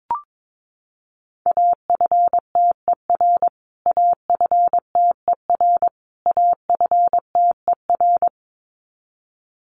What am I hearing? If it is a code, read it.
Morse code, heard as a keyed tone, about 22 wpm, 700 Hz. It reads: AFTER AFTER AFTER
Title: Top 300 Words Repeated 22wpm